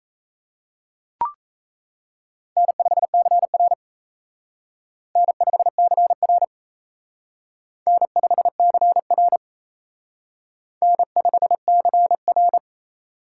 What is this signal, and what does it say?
Morse code, heard as a keyed tone, about 35 wpm, 700 Hz. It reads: N5CR N5CR N5CR N5CR
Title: Speed Racing - USA Call Signs 28wpm